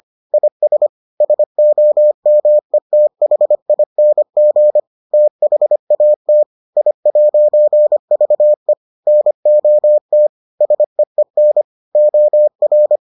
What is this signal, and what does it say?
Morse code, heard as a keyed tone, about 25 wpm, 600 Hz. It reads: IS SOMETHING THAT I'VE NOT SEEN OR